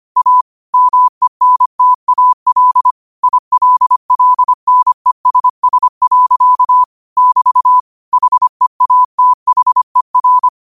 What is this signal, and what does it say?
Morse code, heard as a keyed tone, about 25 wpm, 1000 Hz. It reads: A MENTAL ILLNESS. <BT> HEATHER